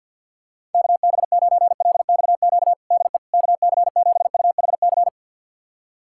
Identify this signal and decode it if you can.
Morse code, heard as a keyed tone, about 50 wpm, 700 Hz. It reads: KB9LXQ DE KC7UHC
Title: USA Callsign DE Callsign - 6 Characters - 50wpm